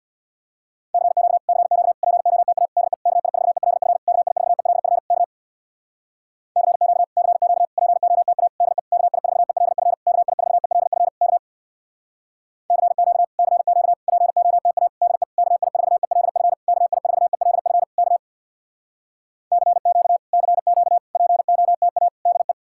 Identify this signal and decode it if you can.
Morse code, heard as a keyed tone, about 66 wpm, 700 Hz. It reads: CQ CQ POTA DE KI4ELF KI4ELF K CQ CQ POTA DE KI4ELF KI4ELF K CQ CQ POTA DE KI4ELF KI4ELF K CQ CQ POTA DE